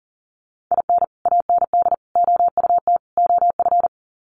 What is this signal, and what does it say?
Morse code, heard as a keyed tone, about 40 wpm, 700 Hz. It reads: IN AND OUT OF
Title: Sets of 4 Words 40wpm